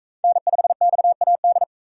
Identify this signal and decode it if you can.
Morse code, heard as a keyed tone, about 38 wpm, 700 Hz. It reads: N5XAD